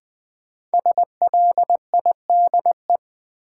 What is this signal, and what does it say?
Morse code, heard as a keyed tone, about 20 wpm, 700 Hz. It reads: SLIDE
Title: Sentences from Top 1600 Words - New Word Drill - 20wpm